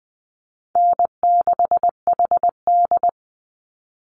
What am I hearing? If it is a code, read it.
Morse code, heard as a keyed tone, about 20 wpm, 700 Hz. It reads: N6HD
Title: Call Signs Repeated 3x - 20wpm